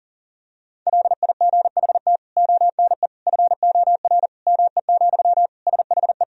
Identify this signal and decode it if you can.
Morse code, heard as a keyed tone, about 40 wpm, 700 Hz. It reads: LIGHT ONE FOR ME, SHE
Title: Sentences from Top 1000 Words - Review Part 1 - 40wpm